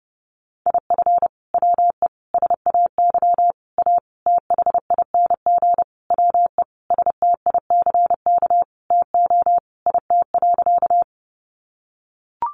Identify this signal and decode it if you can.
Morse code, heard as a keyed tone, about 30 wpm, 700 Hz. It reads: IF WE SAY A THING WE STICK TO IT.
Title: Sentences from Top 1200 Words 30wpm